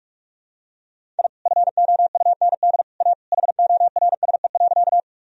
Morse code, heard as a keyed tone, about 46 wpm, 700 Hz. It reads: I FOUND A HORSE.
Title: Sentences from Top 1000 Words - Review Part 1 - 45wpm